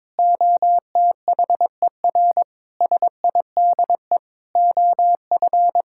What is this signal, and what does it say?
Morse code, heard as a keyed tone, about 22 wpm, 700 Hz. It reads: OTHER SIDE OF
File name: practice / 22wpm / Sentences with 4-Word Phrases - 22wpm